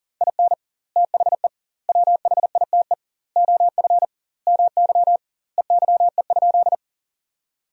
Morse code, heard as a keyed tone, about 41 wpm, 700 Hz. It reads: IN THE WHITE OF MY EYE?
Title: 2x Word-Spacing - Sentences from Top 500 Words - 40wpm